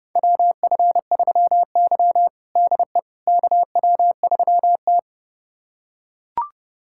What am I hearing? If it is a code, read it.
Morse code, heard as a keyed tone, about 30 wpm, 700 Hz. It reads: WF3Y DE KW3T
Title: USA Callsign DE Callsign - 4 Characters - 30wpm